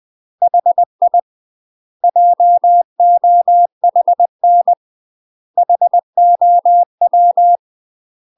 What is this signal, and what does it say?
Morse code, heard as a keyed tone, about 20 wpm, 700 Hz. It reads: HI JOHN HOW
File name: UK MASTER QSOs TRAINING / 20WPM / UK MASTER QSOs 0030